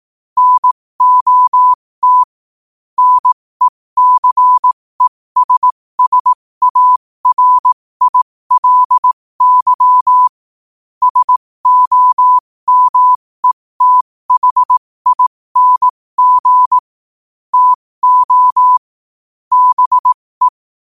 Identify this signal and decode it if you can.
Morse code, heard as a keyed tone, about 18 wpm, spaced as for 15 wpm, 1000 Hz. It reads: NOT NECESSARILY SOMETHING TO BE
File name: qotdfiles / 15wpm